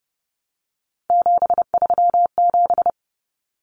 Morse code, heard as a keyed tone, about 30 wpm, 700 Hz. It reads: 737
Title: Numbers - 3-Digit 30wpm